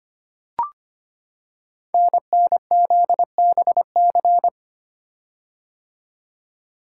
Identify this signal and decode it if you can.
Morse code, heard as a keyed tone, about 25 wpm, 700 Hz. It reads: NNZBC
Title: Code Groups - 5 Letters - 25wpm